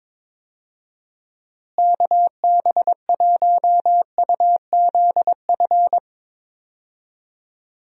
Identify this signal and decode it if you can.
Morse code, heard as a keyed tone, about 22 wpm, 700 Hz. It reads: KB1UZF
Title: USA Call Signs - 6 Characters - Repeated 2x - 22wpm